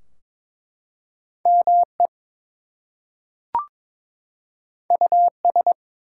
Morse code, heard as a keyed tone, about 22 wpm, 700 Hz. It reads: ME US
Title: Two Letter Words - 22wpm